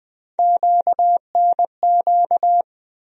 Morse code, heard as a keyed tone, about 20 wpm, 700 Hz. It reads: QNQ
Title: USA Call Sign Postfixes - 3 Letters 20wpm